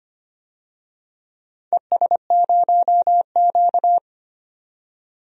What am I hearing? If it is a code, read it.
Morse code, heard as a keyed tone, about 25 wpm, 700 Hz. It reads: ES0Q
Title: DXCC-100 Call Signs - 4 Characters - 25wpm